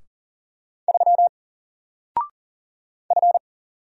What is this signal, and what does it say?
Morse code, heard as a keyed tone, about 38 wpm, 700 Hz. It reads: 3 F
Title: Single Letter-Number 40wpm